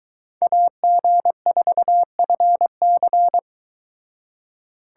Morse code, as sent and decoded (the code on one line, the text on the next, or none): .- --. ....- ..-. -.-.
AG4FC